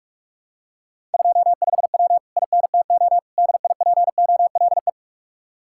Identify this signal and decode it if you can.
Morse code, heard as a keyed tone, about 49 wpm, 700 Hz. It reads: 25W INTO DIPOLE